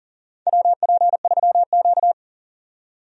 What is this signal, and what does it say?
Morse code, heard as a keyed tone, about 40 wpm, 700 Hz. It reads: WP3Q
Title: USA Call Signs - 4 Characters - 40wpm